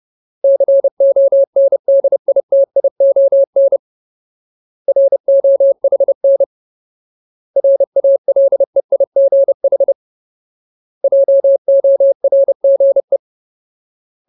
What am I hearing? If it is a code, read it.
Morse code, heard as a keyed tone, about 30 wpm, 550 Hz. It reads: CONDITION ROHN RALEIGH JORGE
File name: Ultimate Word List / 30WPM Ultimate